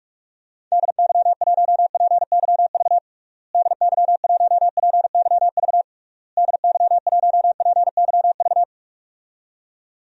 Morse code, heard as a keyed tone, about 46 wpm, 700 Hz. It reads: DY1PYV DY1PYV DY1PYV
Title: DXCC-100 Call Signs - 6 Characters - Repeated 3x - 45wpm